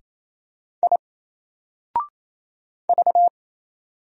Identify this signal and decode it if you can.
Morse code, heard as a keyed tone, about 27 wpm, 700 Hz. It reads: I V
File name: Single-Letter-Number-28wpm